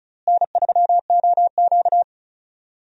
Morse code, heard as a keyed tone, about 34 wpm, 700 Hz. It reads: N3OQ
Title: USA Call Signs - 4 Characters - Repeated 2x - 35wpm